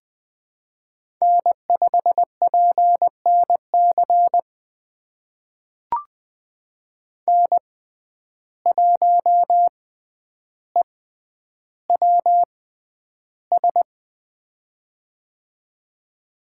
Morse code, heard as a keyed tone, about 20 wpm, spaced as for 10 wpm, 700 Hz. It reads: N5PNC N1EWS
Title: USA Call Signs - 5 Characters - 6x Character Spacing - 20wpm